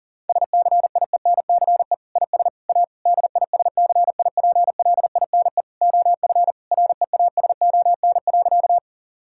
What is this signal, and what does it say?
Morse code, heard as a keyed tone, about 40 wpm, 700 Hz. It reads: SCIENCE IS A DISCIPLINE OF REASON.
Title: Sentences from Top 1000 Words - Review Part 6 - 40wpm